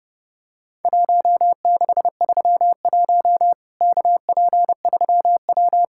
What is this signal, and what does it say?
Morse code, heard as a keyed tone, about 30 wpm, 700 Hz. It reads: 1631 KP3W
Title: Worked All Europe DX Contest - QTC Exchange - 30wpm